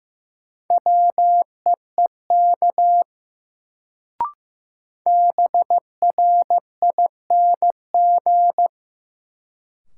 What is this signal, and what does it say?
Morse code, heard as a keyed tone, about 15 wpm, 700 Hz. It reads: WEEK BRING
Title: Sentences from Top 500 Words - New Word Drill - 15wpm